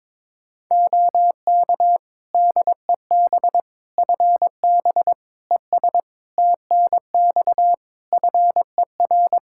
Morse code, heard as a keyed tone, about 22 wpm, 700 Hz. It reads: OK DEB FB ES TNX FER
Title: Practice QSO - Exchange 2 - Info - 22wpm